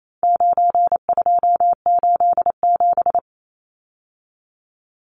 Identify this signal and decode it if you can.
Morse code, heard as a keyed tone, about 28 wpm, 700 Hz. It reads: 9287
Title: Numbers - 4-Digit 28wpm